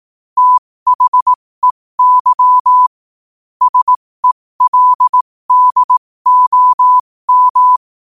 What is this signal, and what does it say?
Morse code, heard as a keyed tone, about 18 wpm, spaced as for 15 wpm, 1000 Hz. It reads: THEY SELDOM
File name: qotdfiles / 15wpm